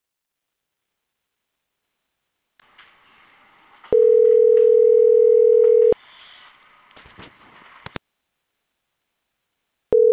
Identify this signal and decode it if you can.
Ringback tone (North America), 440 + 480 Hz.